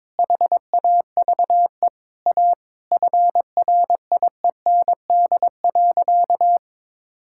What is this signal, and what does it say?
Morse code, heard as a keyed tone, about 22 wpm, 700 Hz. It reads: HAVE A FRIEND.
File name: practice / 22wpm / Sentences with 3-Word Phrases - 22wpm